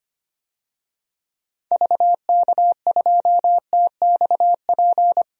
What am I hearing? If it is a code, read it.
Morse code, heard as a keyed tone, about 25 wpm, 700 Hz. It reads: VK2TXP
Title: DXCC-100 Call Signs - 6 Characters - 25wpm